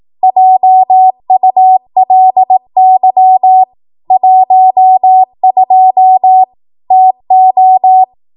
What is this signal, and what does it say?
Morse code, heard as a keyed tone, about 18 wpm, 750 Hz. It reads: JULY 12 TO
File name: Archive / 18 WPM / 18WPM